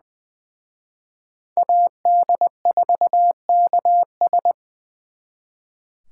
Morse code, heard as a keyed tone, about 24 wpm, 700 Hz. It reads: AD4KS
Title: Speed Racing - USA 2x2 Call Signs 20wpm